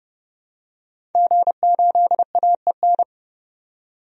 Morse code, heard as a keyed tone, about 30 wpm, 700 Hz. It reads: G8AEN
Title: DXCC-100 Call Signs - 5 Characters - 30wpm